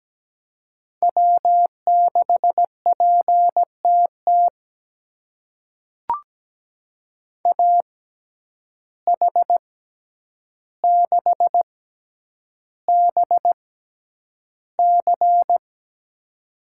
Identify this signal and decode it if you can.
Morse code, heard as a keyed tone, about 17 wpm, spaced as for 9 wpm, 700 Hz. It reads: W6PTT AH6BC
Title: USA Call Signs - 5 Characters - 6x Character Spacing - 17wpm